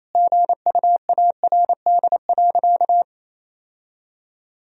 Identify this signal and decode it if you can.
Morse code, heard as a keyed tone, about 28 wpm, 700 Hz. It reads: GUARD.
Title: Sentences from Top 1000 Words - Review Part 4 - 28wpm